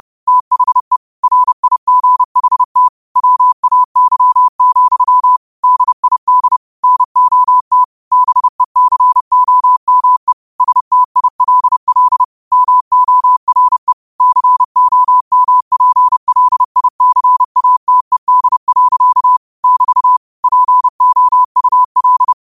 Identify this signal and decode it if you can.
Morse code, heard as a keyed tone, about 30 wpm, 1000 Hz. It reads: THE RIGHT WAY, DID NOT BECOME STILL MORE COMPLICATED. <BT> POUL
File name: qotdfiles / 30wpm